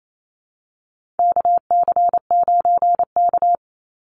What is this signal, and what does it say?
Morse code, heard as a keyed tone, about 28 wpm, 700 Hz. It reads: KC9K
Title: Call Signs Repeated 2x - 28wpm